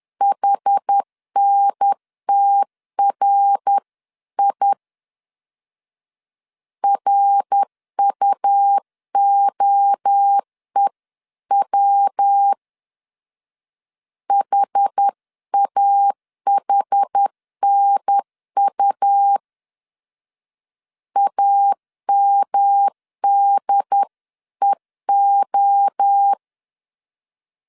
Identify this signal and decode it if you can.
Morse code, heard as a keyed tone, about 10 wpm, 800 Hz. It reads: HNTRI RUOEW HAHNU AMDEO
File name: 10wpm